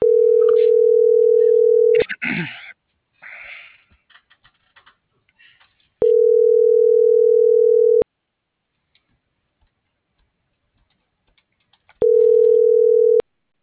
Ringback tone (North America), 440 + 480 Hz.
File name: CENTRAL SAGOT / out edgar.sagot